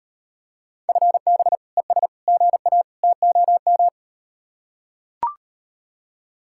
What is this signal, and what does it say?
Morse code, heard as a keyed tone, about 38 wpm, 700 Hz. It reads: FB ES GA TOM